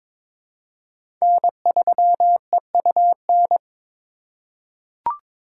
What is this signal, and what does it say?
Morse code, heard as a keyed tone, about 22 wpm, 700 Hz. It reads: N3EUN
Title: USA Call Signs - 2x Character Spacing - 22wpm